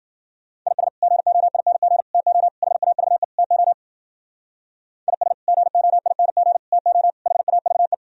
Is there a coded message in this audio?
Yes: Morse code from 0.7 to 7.9 s, heard as a keyed tone, about 56 wpm, 700 Hz. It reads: IS GOING TO HAVE TO IS GOING TO HAVE